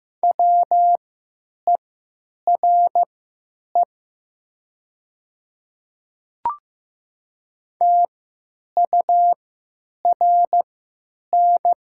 Morse code, heard as a keyed tone, about 15 wpm, spaced as for 9 wpm, 700 Hz. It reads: WERE TURN
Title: 3x Character Spacing - Top 300 Words - 15wpm